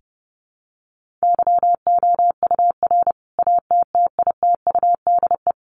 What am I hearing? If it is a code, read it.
Morse code, heard as a keyed tone, about 28 wpm, 700 Hz. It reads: YOUR ATTITUDE
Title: Sentences from Top 1300 Words 30wpm